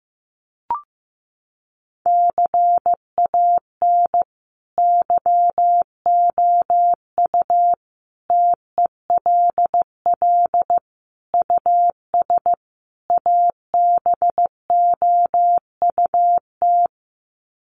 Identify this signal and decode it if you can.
Morse code, heard as a keyed tone, about 15 wpm, 700 Hz. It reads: CAN YOU TELL US ABOUT
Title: Sets of 5 Words Encore2 15wpm